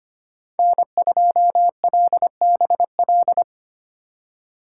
Morse code, heard as a keyed tone, about 25 wpm, 700 Hz. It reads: N2LBL
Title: USA Call Signs - 2x Character Spacing - 25wpm